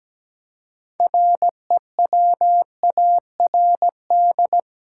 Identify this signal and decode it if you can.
Morse code, heard as a keyed tone, about 17 wpm, 700 Hz. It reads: REWARD
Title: Six Letter Words - 17wpm